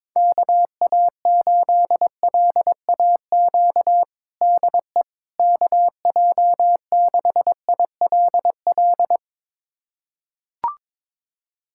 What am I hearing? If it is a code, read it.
Morse code, heard as a keyed tone, about 22 wpm, 700 Hz. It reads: KA8LAQ DE KJ6ILL
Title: USA Callsign DE Callsign 22wpm